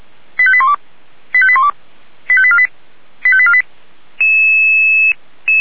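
Five-tone selective call (ZVEI), heard as quick runs of five tones, tones of about 70 ms. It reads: 85812 85812 85848 85848
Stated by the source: ZVEI five-tone selective call received by SDR, Feuerwehr Orlamünde fire brigade.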